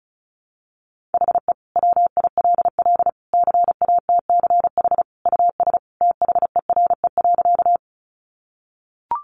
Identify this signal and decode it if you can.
Morse code, heard as a keyed tone, about 34 wpm, 700 Hz. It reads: HE WILL CATCH US THERE.
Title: Sentences from Top 1500 Words 35wpm